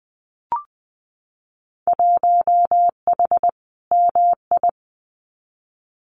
Morse code, heard as a keyed tone, about 20 wpm, 700 Hz. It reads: 1H MI
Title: Winter Field Day Exchanges 20wpm